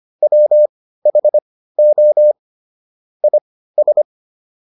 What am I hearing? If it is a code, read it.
Morse code, heard as a keyed tone, about 25 wpm, spaced as for 15 wpm, 600 Hz. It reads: WHO IS